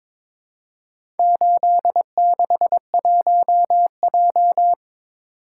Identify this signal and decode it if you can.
Morse code, heard as a keyed tone, about 22 wpm, 700 Hz. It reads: 861J
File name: practice / 22wpm / Four Letter-Number Combinations - 22wpm